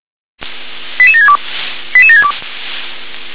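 Five-tone selective call (ZVEI), heard as quick runs of five tones, tones of about 80 ms. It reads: 88762 88762
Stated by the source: ZVEI five-tone call 88553 received on SDR.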